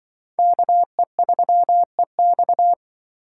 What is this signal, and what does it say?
Morse code, heard as a keyed tone, about 25 wpm, 700 Hz. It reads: KE3EX
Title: Speed Racing - USA Call Signs 20wpm